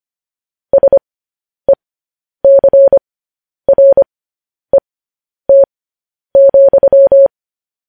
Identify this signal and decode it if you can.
Morse code, heard as a keyed tone, about 25 wpm, spaced as for 10 wpm, 550 Hz. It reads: SECRET,
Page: news-headlines-in-morse-code-at-10-wpm